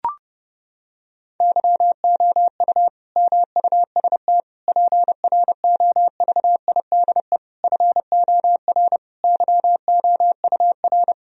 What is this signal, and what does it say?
Morse code, heard as a keyed tone, about 30 wpm, 700 Hz. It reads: YOU MUST PROVIDE FOR YOUR
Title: Sentences from Top 1000 Words - Review Part 3 - 30wpm